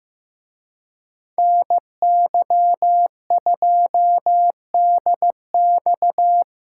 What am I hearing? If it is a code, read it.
Morse code, heard as a keyed tone, about 17 wpm, 700 Hz. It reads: NY2DX
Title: Speed Racing - USA 2x2 Call Signs 15wpm